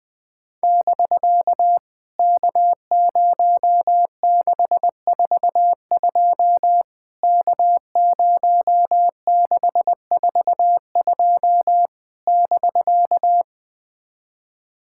Morse code, heard as a keyed tone, about 20 wpm, 700 Hz. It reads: <BK> K0642 K0642 <BK>
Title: POTA QSO Practice - 20wpm